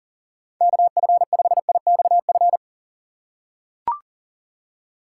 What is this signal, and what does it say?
Morse code, heard as a keyed tone, about 40 wpm, 700 Hz. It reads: KF5IXF
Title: USA Call Signs - 6x Character Spacing - 40wpm